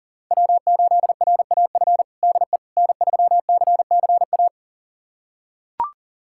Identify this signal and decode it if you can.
Morse code, heard as a keyed tone, about 37 wpm, 700 Hz. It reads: W8RAF DE N3CCA
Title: USA Callsign DE Callsign - 5 Characters - 40wpm